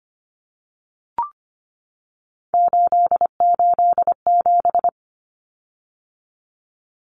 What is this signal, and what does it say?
Morse code, heard as a keyed tone, about 25 wpm, 700 Hz. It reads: 887
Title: Common Cut Numbers - 3-Digit 25wpm